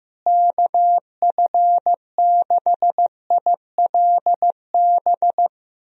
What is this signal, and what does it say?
Morse code, heard as a keyed tone, about 15 wpm, 700 Hz. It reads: KF6ILB
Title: USA Call Signs - 6 Characters - Repeated 2x - 15wpm